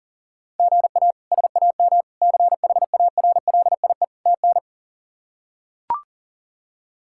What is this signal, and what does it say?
Morse code, heard as a keyed tone, about 40 wpm, 700 Hz. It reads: GA SAM CHARLIE TN
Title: K1USN SST QSO Practice - 40wpm